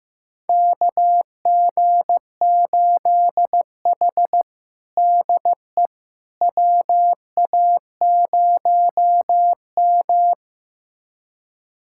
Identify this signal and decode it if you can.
Morse code, heard as a keyed tone, about 15 wpm, 700 Hz. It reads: KG8H DE WA0M
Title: USA Callsign DE Callsign - 4 Characters - 15wpm